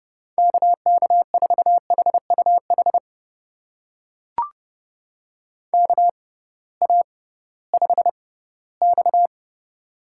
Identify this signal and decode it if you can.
Morse code, heard as a keyed tone, about 30 wpm, spaced as for 18 wpm, 700 Hz. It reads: KK4HUH KA5X
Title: USA Call Signs - 6x Character Spacing - 30wpm